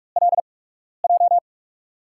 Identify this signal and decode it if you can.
Morse code, heard as a keyed tone, about 42 wpm, spaced as for 12 wpm, 700 Hz. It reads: LJ
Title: USA Call Signs - 6 Characters - 8x Character Spacing - 45wpm